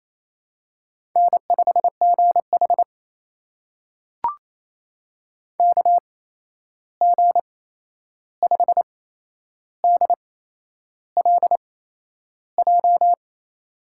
Morse code, heard as a keyed tone, about 27 wpm, spaced as for 10 wpm, 700 Hz. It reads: N5GH KG5DLJ
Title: USA Call Signs - 8x Character Spacing - 28wpm